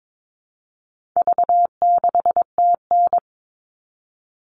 Morse code, heard as a keyed tone, about 22 wpm, 700 Hz. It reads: V6TN
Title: All Cut Numbers - 4-Digit 22wpm